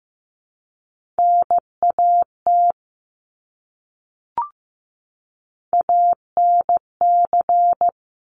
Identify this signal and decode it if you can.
Morse code, heard as a keyed tone, about 15 wpm, 700 Hz. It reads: NAT ANC
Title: Top Three Letter Combinations in Words 15wpm